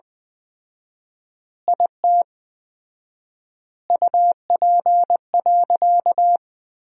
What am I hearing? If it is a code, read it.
Morse code, heard as a keyed tone, about 20 wpm, 700 Hz. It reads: IT UP.
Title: 4x Word-Spacing - Sentences from Top 100 Words - 20wpm